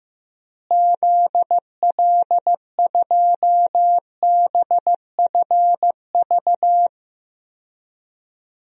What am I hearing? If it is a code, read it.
Morse code, heard as a keyed tone, about 16 wpm, 700 Hz. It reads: ZL2BFV
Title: DXCC-100 Call Signs - 6 Characters - Repeated 3x - 15wpm